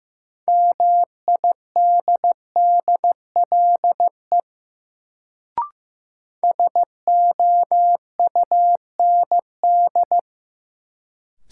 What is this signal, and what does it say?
Morse code, heard as a keyed tone, about 15 wpm, 700 Hz. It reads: MIDDLE SOUND
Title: Sentences from Top 900 Words - New Word Drill - 15wpm